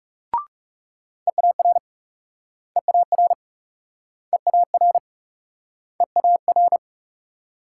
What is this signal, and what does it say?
Morse code, heard as a keyed tone, about 40 wpm, 700 Hz. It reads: EAR EAR EAR EAR